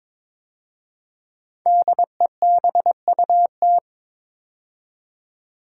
Morse code, heard as a keyed tone, about 22 wpm, 700 Hz. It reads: DEBUT
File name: 5 Letter Words - First 3 Letters Given - 22wpm